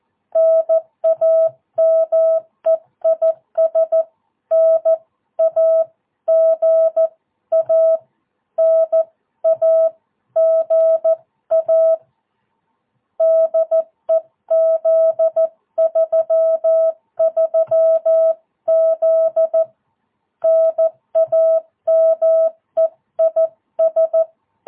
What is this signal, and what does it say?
Morse code, heard as a keyed tone, about 14 wpm, 650 Hz. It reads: NAME IS NAGA NAGA DE Z33Z NAME IS
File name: jh8cbh / 2022cw